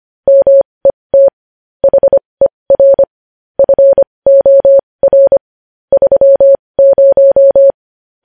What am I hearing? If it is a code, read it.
Morse code, heard as a keyed tone, about 25 wpm, spaced as for 20 wpm, 550 Hz. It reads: MET HER FOR 30
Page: News Headlines in Morse Code at 20 WPM